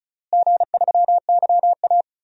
Morse code, heard as a keyed tone, about 34 wpm, 700 Hz. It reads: G3YA
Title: DXCC-100 Call Signs - 4 Characters - Repeated 3x - 35wpm